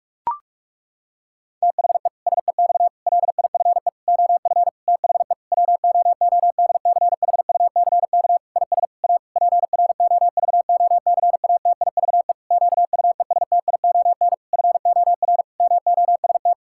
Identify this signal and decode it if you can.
Morse code, heard as a keyed tone, about 46 wpm, 700 Hz. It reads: THE SEX LIFE OF THE WOODCHUCK IS A PROVOCATIVE QUESTION FOR MOST
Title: Pangram Challenge v2 - 45wpm